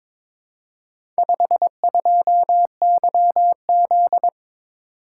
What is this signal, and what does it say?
Morse code, heard as a keyed tone, about 23 wpm, 700 Hz. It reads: 52YZ
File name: practice / 22wpm / Four Letter-Number Combinations - 22wpm